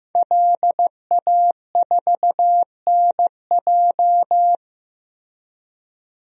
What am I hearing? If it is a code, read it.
Morse code, heard as a keyed tone, about 15 wpm, 700 Hz. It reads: LA4NJ
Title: DXCC-100 Call Signs - 5 Characters - Repeated 2x - 15wpm